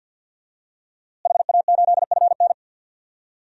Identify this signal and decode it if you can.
Morse code, heard as a keyed tone, about 52 wpm, 700 Hz. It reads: HA8FN
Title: DXCC-100 Call Signs - 5 Characters - Repeated 3x - 50wpm